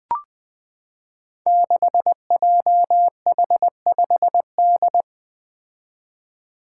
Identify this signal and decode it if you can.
Morse code, heard as a keyed tone, about 20 wpm, 700 Hz. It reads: 6JH5D
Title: Five Challenging Letter-Number Combinations 20wpm